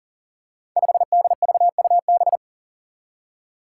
Morse code, heard as a keyed tone, about 40 wpm, 700 Hz. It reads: 5DVUB
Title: Five Challenging Letter-Number Combinations 40wpm